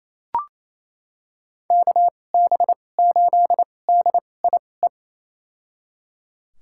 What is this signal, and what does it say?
Morse code, heard as a keyed tone, about 28 wpm, spaced as for 20 wpm, 700 Hz. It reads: KB8DIE
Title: USA Call Signs - 6 Characters - 2x Character Spacing - 28wpm